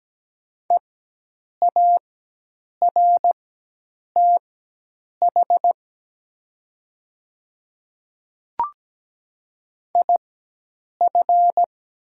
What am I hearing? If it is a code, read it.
Morse code, heard as a keyed tone, about 17 wpm, spaced as for 8 wpm, 700 Hz. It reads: EARTH IF
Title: 4x Character Spacing - Top 200 Words - 17wpm